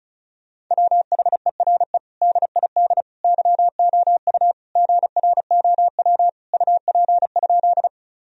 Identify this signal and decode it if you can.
Morse code, heard as a keyed tone, about 34 wpm, 700 Hz. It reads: WHERE DID YOU GROW UP?
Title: Sentences from Top 1000 Words - Review Part 1 - 35wpm